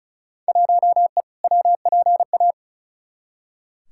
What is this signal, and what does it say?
Morse code, heard as a keyed tone, about 34 wpm, 700 Hz. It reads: 1E WPA
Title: Field Day Exchanges - 35wpm